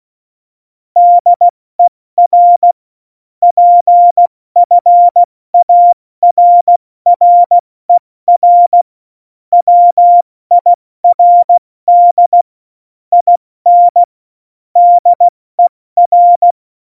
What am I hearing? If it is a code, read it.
Morse code, heard as a keyed tone, about 16 wpm, 700 Hz. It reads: DER PFARRER WIRD IN DER
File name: Morsekurs / Witze / Witz 09. Oktober 2018, 70 F80